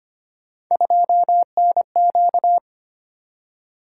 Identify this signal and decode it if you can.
Morse code, heard as a keyed tone, about 25 wpm, 700 Hz. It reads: 2NQ
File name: Three Letter-Number Combinations - 25wpm